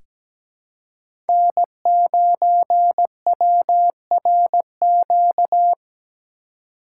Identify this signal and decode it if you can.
Morse code, heard as a keyed tone, about 17 wpm, 700 Hz. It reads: N9WRQ
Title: USA Call Signs - 5 Characters - 3x Character Spacing - 17wpm